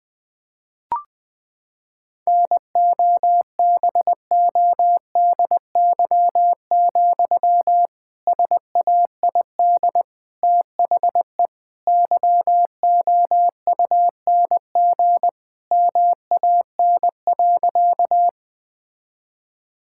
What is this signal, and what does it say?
Morse code, heard as a keyed tone, about 21 wpm, 700 Hz. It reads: NOBODY, SAID THE YOUNG MAN.
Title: Sentences from Top 1600 Words 20wpm